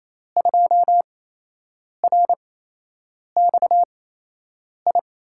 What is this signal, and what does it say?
Morse code, heard as a keyed tone, about 28 wpm, spaced as for 8 wpm, 700 Hz. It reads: 2RXI
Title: USA Call Signs - 6 Characters - 8x Character Spacing - 28wpm